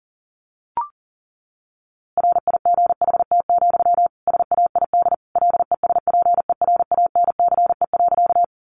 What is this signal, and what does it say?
Morse code, heard as a keyed tone, about 40 wpm, 700 Hz. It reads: RIGHT, SAID LESPERANCE.
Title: A Sound Of Thunder 40wpm